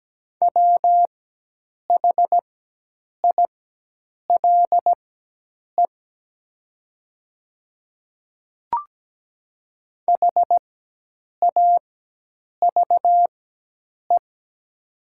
Morse code, heard as a keyed tone, about 17 wpm, spaced as for 8 wpm, 690 Hz. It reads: WHILE HAVE
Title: 4x Character Spacing - Top 300 Words - 17wpm